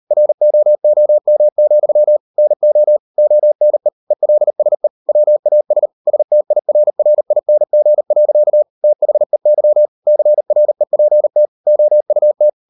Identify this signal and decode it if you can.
Morse code, heard as a keyed tone, about 38 wpm, 600 Hz. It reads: ROOM, NO ONE ELSE WAS STIRRING. THEY CREPT OUT